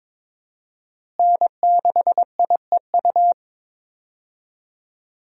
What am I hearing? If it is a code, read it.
Morse code, heard as a keyed tone, about 22 wpm, 700 Hz. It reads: N6IEU
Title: USA Call Signs - 5 Characters - Repeated 2x - 22wpm